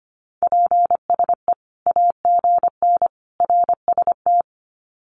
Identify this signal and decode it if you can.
Morse code, heard as a keyed tone, about 25 wpm, 700 Hz. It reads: PSE AGN RST
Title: Common QSO Phrases 25wpm